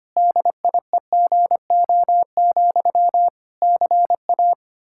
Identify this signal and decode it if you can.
Morse code, heard as a keyed tone, about 25 wpm, 700 Hz. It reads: DIEGO, CA
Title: Practice QSO - Exchange 1 - Beginning - 25wpm